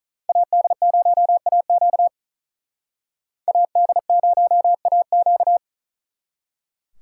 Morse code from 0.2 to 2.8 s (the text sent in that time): AD0AQ